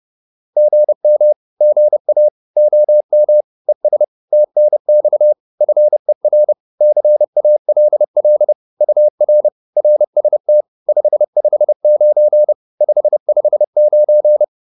Morse code, heard as a keyed tone, about 30 wpm, 600 Hz. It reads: GM GA OM ES TNX FER CALL UR RST 559 559
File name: QSO410 30